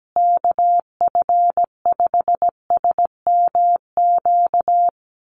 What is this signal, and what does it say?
Morse code, heard as a keyed tone, about 17 wpm, 700 Hz. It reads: KF5SMQ
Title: USA Call Signs Repeated 3x 17wpm